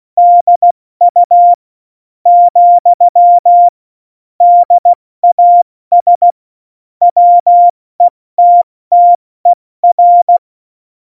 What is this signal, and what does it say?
Morse code, heard as a keyed tone, about 16 wpm, 700 Hz. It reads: DU , DAS WETTER
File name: Morsekurs / Witze / Witz 14. August 2018, 70 F80